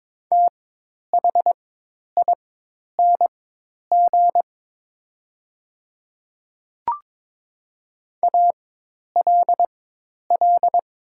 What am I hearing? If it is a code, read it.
Morse code, heard as a keyed tone, about 22 wpm, spaced as for 10 wpm, 690 Hz. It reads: THING ALL
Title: 4x Character Spacing - Top 200 Words - 22wpm